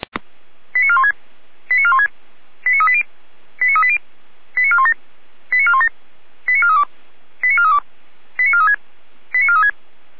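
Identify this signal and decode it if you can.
Five-tone selective call (ZVEI), heard as quick runs of five tones, tones of about 70 ms. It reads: 89416 89416 89380 89380 89417 89417 89432 89432 89436 89436
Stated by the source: ZVEI five-tone selective call, received by SDR.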